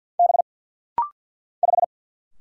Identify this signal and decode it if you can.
Morse code, heard as a keyed tone, about 50 wpm, 700 Hz. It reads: B 5
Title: Single Letter-Number - Rapid-Fire 50wpm